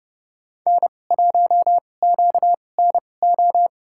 Morse code, heard as a keyed tone, about 30 wpm, spaced as for 22 wpm, 700 Hz. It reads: N1QNO
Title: USA Call Signs - 5 Characters - 2x Character Spacing - 30wpm